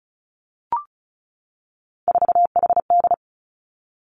Morse code, heard as a keyed tone, about 36 wpm, 700 Hz. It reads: 4HD
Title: Three Challenging Letter-Number Combinations 35wpm